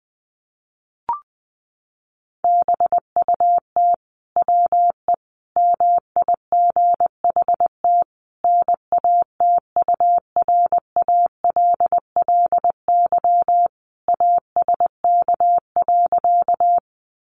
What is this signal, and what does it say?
Morse code, heard as a keyed tone, about 20 wpm, 700 Hz. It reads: BUT WE MIGHT NATURALLY ASK.
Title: Sentences from Top 1300 Words 20wpm